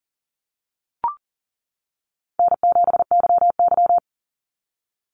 Morse code, heard as a keyed tone, about 40 wpm, 700 Hz. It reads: N7YY
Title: NA SOTA Chaser Callsigns 40wpm